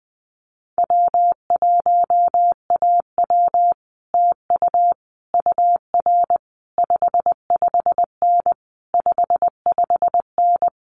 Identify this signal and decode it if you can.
Morse code, heard as a keyed tone, about 20 wpm, 700 Hz. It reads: W1AW TU UR 55N 55N
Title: Common QSO Phrases Encore 20wpm